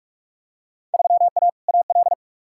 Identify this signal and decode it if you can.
Morse code, heard as a keyed tone, about 47 wpm, 700 Hz. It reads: 3A AL
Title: Field Day Exchanges - 45wpm